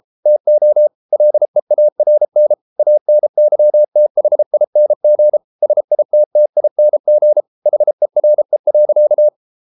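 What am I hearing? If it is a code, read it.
Morse code, heard as a keyed tone, about 33 wpm, 600 Hz. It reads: TO LEARN ANYTHING SITTING HERE.
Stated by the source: Morse code practice at 32 wpm from CW med Gnister, reading Ken Follett's novel Never in English.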